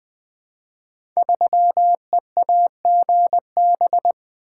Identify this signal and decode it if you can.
Morse code, heard as a keyed tone, about 20 wpm, 700 Hz. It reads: 3EAGB
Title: Code Groups - 5 Letter:Numbers - 20wpm